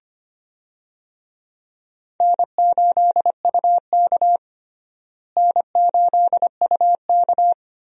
Morse code, heard as a keyed tone, about 25 wpm, 700 Hz. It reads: N8UK N8UK
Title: USA Call Signs - 4 Characters - Repeated 3x - 25wpm